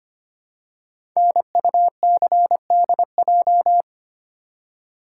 Morse code, heard as a keyed tone, about 25 wpm, 700 Hz. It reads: NUCDJ
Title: Code Groups - 5 Letters - 25wpm